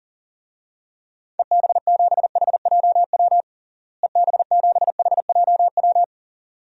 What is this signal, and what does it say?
Morse code, heard as a keyed tone, about 40 wpm, 700 Hz. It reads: EB7HJW EB7HJW
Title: DXCC-100 Call Signs - 6 Characters - Repeated 2x - 40wpm